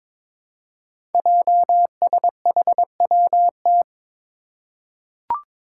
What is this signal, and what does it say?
Morse code, heard as a keyed tone, about 22 wpm, 700 Hz. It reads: JSHWT
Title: Five Challenging Letter-Number Combinations 22wpm